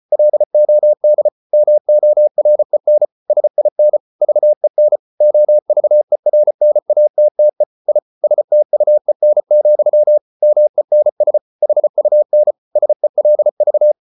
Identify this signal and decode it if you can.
Morse code, heard as a keyed tone, about 34 wpm, 600 Hz. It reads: LOD MOREN SIN VEN OVERNATTE I STUEN, MENS HUN SELV
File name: Ferie QRQ 34wpm